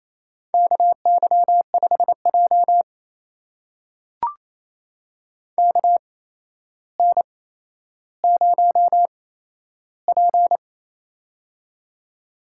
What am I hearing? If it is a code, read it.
Morse code, heard as a keyed tone, about 27 wpm, spaced as for 12 wpm, 700 Hz. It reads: KY5J KN0P
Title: USA Call Signs - 4 Characters - 8x Character Spacing - 28wpm